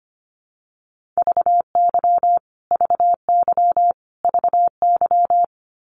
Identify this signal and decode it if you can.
Morse code, heard as a keyed tone, about 25 wpm, 700 Hz. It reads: VY VY VY
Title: Abbreviations Repeated 3x - 25wpm